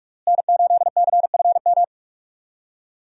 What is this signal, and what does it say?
Morse code, heard as a keyed tone, about 47 wpm, 700 Hz. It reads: N8CFM
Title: USA Call Signs - 5 Characters - Repeated 3x - 45wpm